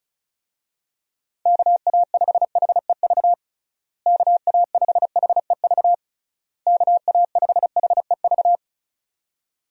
Morse code, heard as a keyed tone, about 34 wpm, 700 Hz. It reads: KA5HEV KA5HEV KA5HEV
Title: USA Call Signs - 6 Characters - Repeated 3x - 35wpm